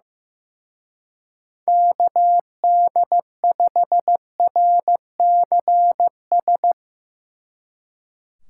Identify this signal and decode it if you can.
Morse code, heard as a keyed tone, about 15 wpm, 700 Hz. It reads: KD5RCS
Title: Speed Racing - USA 2x3 Call Signs 15wpm